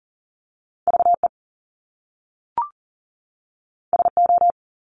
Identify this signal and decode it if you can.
Morse code, heard as a keyed tone, about 40 wpm, 700 Hz. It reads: VE SO
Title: Top Two Letter Combinations in Words 40wpm